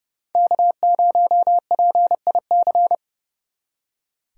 Morse code, heard as a keyed tone, about 37 wpm, 700 Hz. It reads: K0PIC
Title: Speed Racing - USA Call Signs 30wpm